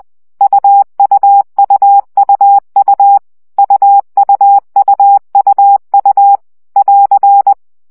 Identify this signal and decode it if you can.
Morse code, heard as a keyed tone, about 20 wpm, 800 Hz. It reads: UUUUU UUUUU <AR>
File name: CW gyakorlas u 17 baud